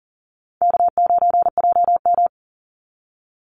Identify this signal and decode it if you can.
Morse code, heard as a keyed tone, about 40 wpm, 700 Hz. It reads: K9JM
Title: Call Signs Repeated 2x - 40wpm